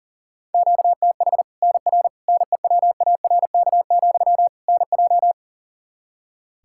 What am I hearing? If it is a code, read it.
Morse code, heard as a keyed tone, about 40 wpm, 700 Hz. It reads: QTH NR NEWARK, NJ
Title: Practice QSO - Exchange 1 - Beginning - 40wpm